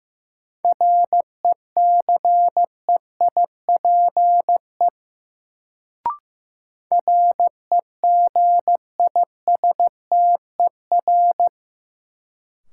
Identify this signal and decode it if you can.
Morse code, heard as a keyed tone, about 15 wpm, 700 Hz. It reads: RECEIPE REGISTER
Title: Sentences from Top 1200 Words - New Word Drill - 15wpm